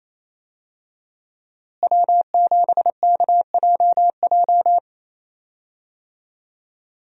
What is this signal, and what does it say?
Morse code, heard as a keyed tone, about 28 wpm, 700 Hz. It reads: W7KJJ
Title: USA Call Signs - 5 Characters - Repeated 2x - 28wpm